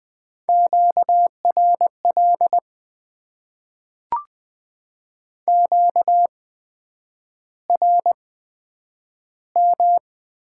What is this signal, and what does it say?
Morse code, heard as a keyed tone, about 20 wpm, spaced as for 9 wpm, 700 Hz. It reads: QRL QRM
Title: Top 90 QSO Elements - 8x Character Spacing - 20wpm